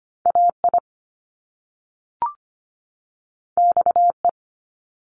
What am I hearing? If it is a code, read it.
Morse code, heard as a keyed tone, about 25 wpm, 700 Hz. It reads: AI XE
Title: Top Two Letter Combinations in Words 25wpm